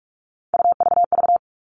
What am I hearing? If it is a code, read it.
Morse code, heard as a keyed tone, about 46 wpm, 700 Hz. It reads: UVV
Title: Three Challenging Letter-Number Combinations 45wpm